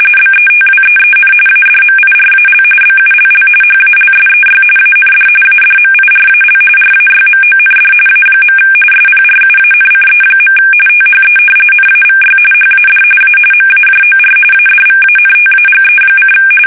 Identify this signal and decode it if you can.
RTTY, about 45.45 baud. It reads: BETWEEN ADMIRAL BULL HALSEY AND ADMIRAL RAYMOND A. SPRUANCE. THE NAVY HAD A MAJOR ADVANTAGE:
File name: fsk850